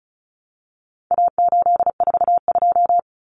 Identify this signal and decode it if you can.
Morse code, heard as a keyed tone, about 34 wpm, 700 Hz. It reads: A842
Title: Common Cut Numbers - 4-Digit 35wpm